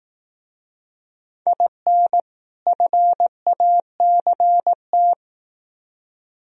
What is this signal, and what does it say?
Morse code, heard as a keyed tone, about 18 wpm, 700 Hz. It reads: IN FACT
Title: Speed-Racing - Sets of 2 Words 15wpm